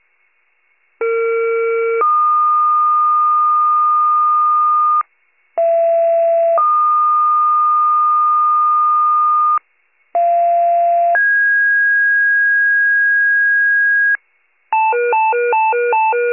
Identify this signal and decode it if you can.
Two-tone page, heard as a one-second tone, then a three-second tone, about 460 then 1220 Hz, about 670 then 1220 Hz, about 690 then 1650 Hz. It is heard from 1.0 to 14.2 s.